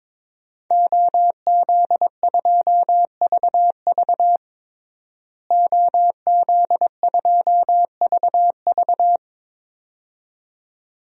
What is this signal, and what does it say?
Morse code, heard as a keyed tone, about 21 wpm, 700 Hz. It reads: OZ2VV OZ2VV
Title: DXCC-100 Call Signs - 5 Characters - Repeated 2x - 22wpm